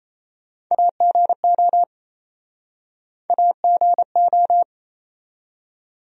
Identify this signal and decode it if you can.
Morse code, heard as a keyed tone, about 37 wpm, 700 Hz. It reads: AGO AGO